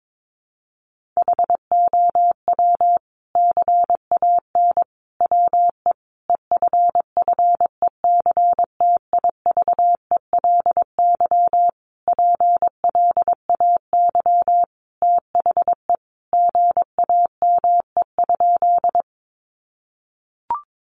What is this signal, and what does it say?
Morse code, heard as a keyed tone, about 22 wpm, 700 Hz. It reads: HOW CAN WE EFFECTIVELY PLAY THE GAME?
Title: Sentences from Top 1400 Words 22wpm